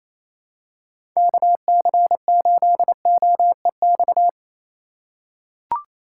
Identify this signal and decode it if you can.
Morse code, heard as a keyed tone, about 28 wpm, 700 Hz. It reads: KC8OEX
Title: USA Call Signs - 6 Characters - 4x Character Spacing - 28wpm